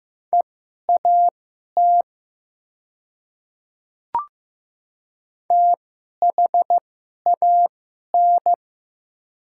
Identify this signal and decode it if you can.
Morse code, heard as a keyed tone, about 15 wpm, spaced as for 11 wpm, 700 Hz. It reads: EAT THAN
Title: 2x Character Spacing - Top 300 Words - 15wpm